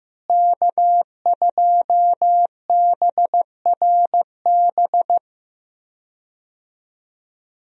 Morse code, heard as a keyed tone, about 15 wpm, 700 Hz. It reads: K2BRB
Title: USA Call Signs - 5 Characters - Repeated 2x - 15wpm